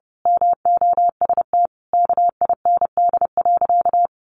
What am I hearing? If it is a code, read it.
Morse code, heard as a keyed tone, about 30 wpm, 700 Hz. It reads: MOST KIND.
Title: Sentences from Top 200 Words 30wpm